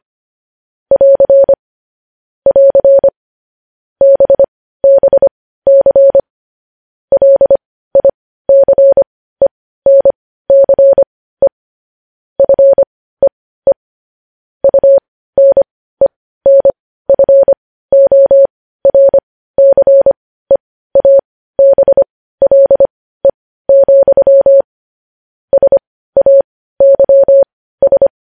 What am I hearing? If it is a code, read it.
Morse code, heard as a keyed tone, about 25 wpm, spaced as for 15 wpm, 550 Hz. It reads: <AR> <AR> BBC LICENCE FEE UNENFORCEABLE, SAYS
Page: news-headlines-in-morse-code-at-15-wpm